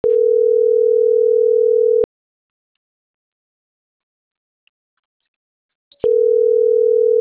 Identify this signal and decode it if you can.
Ringback tone (North America), 440 + 480 Hz.